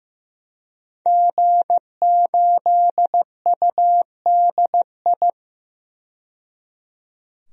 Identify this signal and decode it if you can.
Morse code, heard as a keyed tone, about 15 wpm, 700 Hz. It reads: G8UDI
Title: DXCC-100 Call Signs - 5 Characters - 15wpm